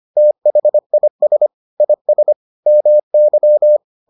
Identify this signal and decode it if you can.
Morse code, heard as a keyed tone, about 25 wpm, 600 Hz. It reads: THIS IS MY